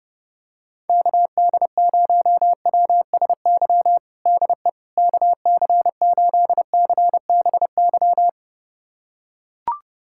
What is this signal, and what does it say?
Morse code, heard as a keyed tone, about 30 wpm, 700 Hz. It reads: KD0WSY DE KC8CBY
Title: USA Callsign DE Callsign 30wpm